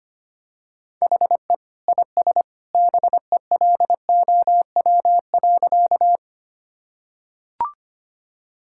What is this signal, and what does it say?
Morse code, heard as a keyed tone, about 25 wpm, 700 Hz. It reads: HE IS BELOW.
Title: Sentences from Top 1000 Words - Review Part 5 - 25wpm